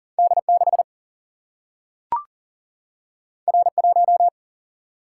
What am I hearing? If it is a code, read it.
Morse code, heard as a keyed tone, about 39 wpm, 700 Hz. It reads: D 6 R 1
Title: Single-Letter Single-Number 40wpm